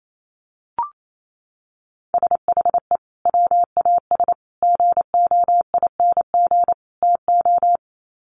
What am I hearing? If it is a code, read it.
Morse code, heard as a keyed tone, about 28 wpm, 700 Hz. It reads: SHE WAS GOING TO
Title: Sets of 4 Words 28wpm